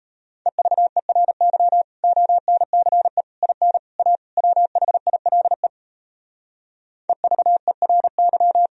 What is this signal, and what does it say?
Morse code, heard as a keyed tone, about 37 wpm, 700 Hz. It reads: EVERY ONCE IN A WHILE EVERY